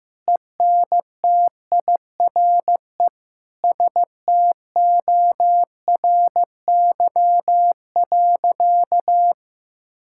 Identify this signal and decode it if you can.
Morse code, heard as a keyed tone, about 15 wpm, 700 Hz. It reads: ENTIRE STORY.
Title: Sentences from Top 1000 Words - Review Part 5 - 15wpm